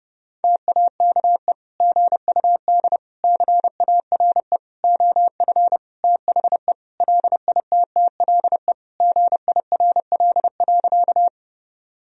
Morse code, heard as a keyed tone, about 30 wpm, 700 Hz. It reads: TAKE GUD CARE OF THE LITTLE GIRL.
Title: Sentences from Top 400 Words and Abbreviations 30wpm